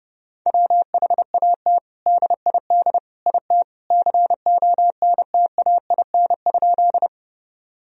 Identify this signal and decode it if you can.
Morse code, heard as a keyed tone, about 30 wpm, 700 Hz. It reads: WHAT DID IT CONTAIN?
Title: Sentences from Top 1000 Words - Review Part 2 - 30wpm